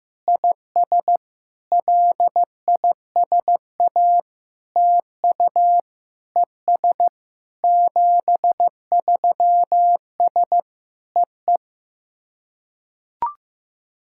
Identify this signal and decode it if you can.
Morse code, heard as a keyed tone, about 15 wpm, 700 Hz. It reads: IS LISA TU ES 73S EE
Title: SOTA QSO Practice 15wpm